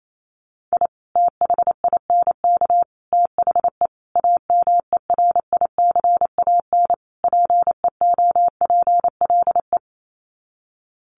Morse code, heard as a keyed tone, about 28 wpm, 700 Hz. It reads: I THINK THE AMERICAN PEOPLE
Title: Sets of 5 Words Encore 28wpm